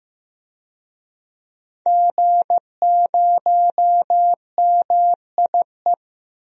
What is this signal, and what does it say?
Morse code, heard as a keyed tone, about 15 wpm, 700 Hz. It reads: G0MIE
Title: DXCC-100 Call Signs - 5 Characters - Repeated 3x - 15wpm